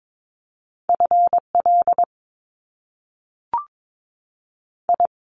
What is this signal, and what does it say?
Morse code, heard as a keyed tone, about 22 wpm, 700 Hz. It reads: F L I
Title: Top Two Letter Combinations in Words 22wpm